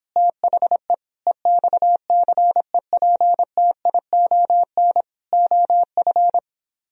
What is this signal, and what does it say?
Morse code, heard as a keyed tone, about 26 wpm, 700 Hz. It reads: THE EXCEPTION OF